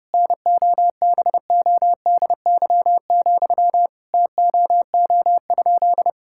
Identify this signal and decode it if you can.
Morse code, heard as a keyed tone, about 30 wpm, 700 Hz. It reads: NOBODY, TOO?
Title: Sentences from Top 1600 Words 30wpm